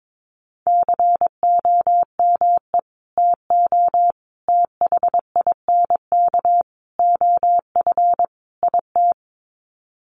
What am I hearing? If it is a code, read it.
Morse code, heard as a keyed tone, about 22 wpm, 700 Hz. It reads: COME TO THINK OF IT
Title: Sets of 5 Words Encore 22wpm